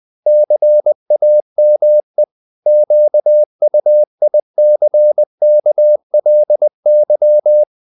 Morse code, heard as a keyed tone, about 20 wpm, 600 Hz. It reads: CAME QUICKLY
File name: the old man and the sea / the old man and the sea part 1 20wpm